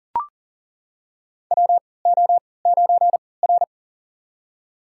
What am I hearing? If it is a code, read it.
Morse code, heard as a keyed tone, about 40 wpm, spaced as for 23 wpm, 700 Hz. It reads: WO9R
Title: USA Call Signs - 3x Character Spacing - 40wpm